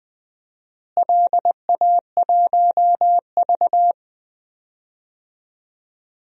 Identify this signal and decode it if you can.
Morse code, heard as a keyed tone, about 20 wpm, 700 Hz. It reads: LA1V
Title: DXCC-100 Call Signs - 4 Characters - Repeated 3x - 20wpm